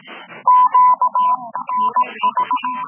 Morse code, heard as a keyed tone, about 18 wpm, 1000 Hz. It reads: Q NH